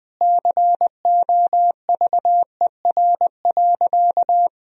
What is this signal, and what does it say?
Morse code, heard as a keyed tone, about 21 wpm, 700 Hz. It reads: COVER.
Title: Sentences from Top 1000 Words - Review Part 1 - 20wpm